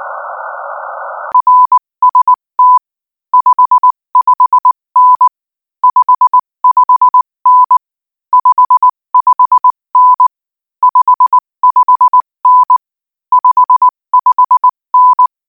Morse code, heard as a keyed tone, about 19 wpm, 1000 Hz. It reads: RST 55N 55N 55N 55N 55N
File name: IZ4BEH-13-cm-AS